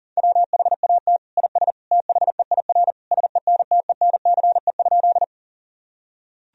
Morse code, heard as a keyed tone, about 40 wpm, 700 Hz. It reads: WHAT IS THEIR SENTENCE?
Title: Sentences from Top 1000 Words - Review Part 1 - 40wpm